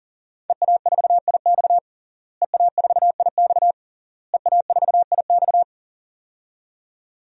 Morse code, heard as a keyed tone, about 40 wpm, 700 Hz. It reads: EA4IX EA4IX EA4IX
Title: DXCC-100 Call Signs - 5 Characters - Repeated 3x - 40wpm